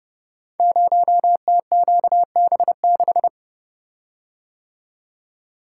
Morse code, heard as a keyed tone, about 30 wpm, 700 Hz. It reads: 0TQB6
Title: Code Groups - 5 Letter:Numbers - 30wpm